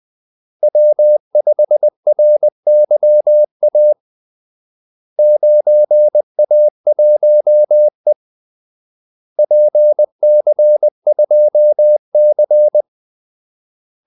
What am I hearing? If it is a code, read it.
Morse code, heard as a keyed tone, about 20 wpm, 600 Hz. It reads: W5RYA 9A1E PC2C
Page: Kaldesignaler 20 wpm